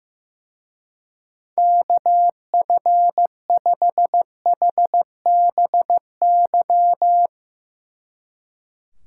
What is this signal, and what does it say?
Morse code, heard as a keyed tone, about 16 wpm, 700 Hz. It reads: KF5HBY
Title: Speed Racing - USA Call Signs 15wpm